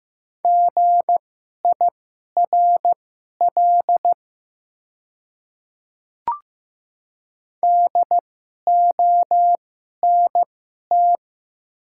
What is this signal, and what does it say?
Morse code, heard as a keyed tone, about 15 wpm, spaced as for 11 wpm, 700 Hz. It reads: GIRL DONT
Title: 2x Character Spacing - Top 500 Words - 15wpm